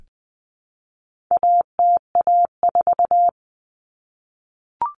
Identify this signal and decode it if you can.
Morse code, heard as a keyed tone, about 20 wpm, 700 Hz. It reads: ATA4
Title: Common Cut Numbers - 4-Digit 20wpm